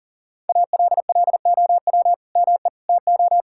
Morse code, heard as a keyed tone, about 37 wpm, 700 Hz. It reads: ALLOW ME TO
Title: Sentences from Top 1000 Words - Review Part 5 - 40wpm